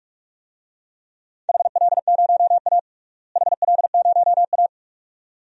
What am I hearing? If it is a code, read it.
Morse code, heard as a keyed tone, about 42 wpm, 700 Hz. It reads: HL0A HL0A
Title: DXCC-100 Call Signs - 4 Characters - Repeated 2x - 45wpm